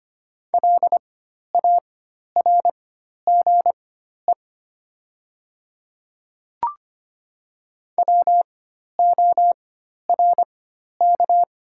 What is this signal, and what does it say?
Morse code, heard as a keyed tone, about 25 wpm, spaced as for 12 wpm, 700 Hz. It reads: LARGE WORK
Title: 4x Character Spacing - Top 200 Words - 25wpm